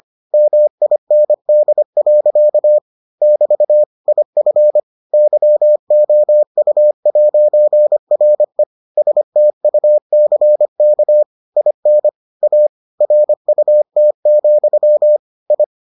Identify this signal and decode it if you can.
Morse code, heard as a keyed tone, about 25 wpm, 600 Hz. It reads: MIND. <BT> IF YOU'RE STUCK IN A RUT, I